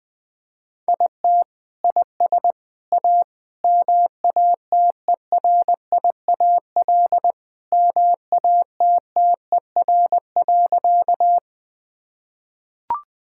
Morse code, heard as a keyed tone, about 21 wpm, 700 Hz. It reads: IT IS A MATERIAL MATTER.
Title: Sentences from Top 1000 Words - Review Part 5 - 20wpm